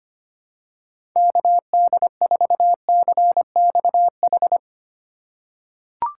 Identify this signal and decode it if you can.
Morse code, heard as a keyed tone, about 25 wpm, 700 Hz. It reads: KD4CXH
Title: USA Call Signs - 6 Characters - 3x Character Spacing - 25wpm